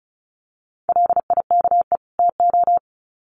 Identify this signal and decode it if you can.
Morse code, heard as a keyed tone, about 34 wpm, 700 Hz. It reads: LIKE TO
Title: Sets of 2 Words 35wpm